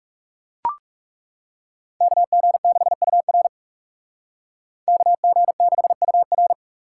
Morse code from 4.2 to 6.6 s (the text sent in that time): KG6UR